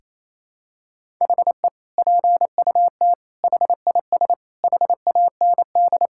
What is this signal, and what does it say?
Morse code, heard as a keyed tone, about 28 wpm, 700 Hz. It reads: HE PUT HIS HAND
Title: Sentences from Top 1000 Words - Review Part 6 - 28wpm